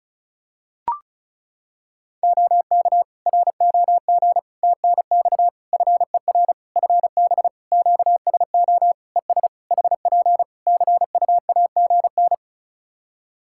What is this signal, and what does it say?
Morse code, heard as a keyed tone, about 36 wpm, 700 Hz. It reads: OK ROG TNX FER FB QSO ES HP CUAGN
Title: Practice QSO - Exchange 4 - Ending - 35wpm